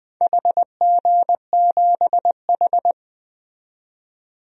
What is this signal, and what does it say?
Morse code, heard as a keyed tone, about 21 wpm, 700 Hz. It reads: HG7H
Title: DXCC-100 Call Signs - 4 Characters - 20wpm